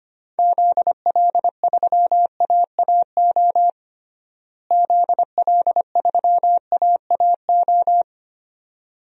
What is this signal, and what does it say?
Morse code, heard as a keyed tone, about 25 wpm, 700 Hz. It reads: ZL3AAO ZL3AAO
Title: DXCC-100 Call Signs - 6 Characters - Repeated 2x - 25wpm